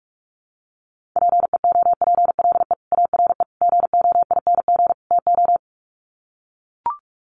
Morse code, heard as a keyed tone, about 44 wpm, 700 Hz. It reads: PEOPLE ARE GOING TO
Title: Sets of 4 Words 45wpm